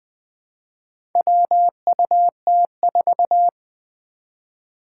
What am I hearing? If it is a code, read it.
Morse code, heard as a keyed tone, about 20 wpm, 700 Hz. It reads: WUT4
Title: Four Challenging Letter-Number Combinations 20wpm